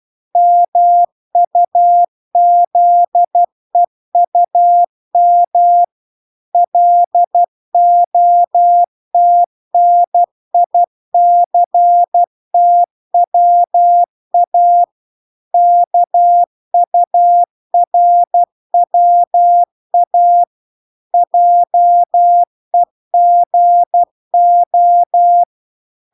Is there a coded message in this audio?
Yes: Morse code from 0.4 to 25.4 s, heard as a keyed tone, about 12 wpm, 700 Hz. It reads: MUZEUM LOTNICTWA KURWA JEGO